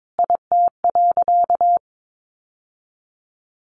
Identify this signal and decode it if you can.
Morse code, heard as a keyed tone, about 22 wpm, 700 Hz. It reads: IT.
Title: Sentences from Top 800 Words 22wpm